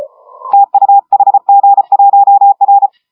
Morse code, heard as a keyed tone, about 32 wpm, 810 Hz, with radio noise. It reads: TU HG1R
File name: AUDIO2019 / HG1R CW SSB MIX